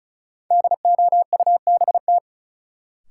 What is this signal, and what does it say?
Morse code, heard as a keyed tone, about 36 wpm, 700 Hz. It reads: DOUBT
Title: Sentences from Top 1500 Words - New Word Drill - 35wpm